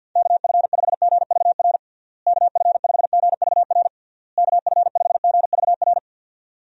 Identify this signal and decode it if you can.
Morse code, heard as a keyed tone, about 51 wpm, 700 Hz. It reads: KF5GVR KF5GVR KF5GVR
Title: USA Call Signs - 6 Characters - Repeated 3x - 50wpm